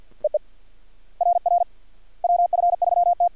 Morse code, heard as a keyed tone, about 51 wpm, 700 Hz. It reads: RR WF3T
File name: archived / CRSnet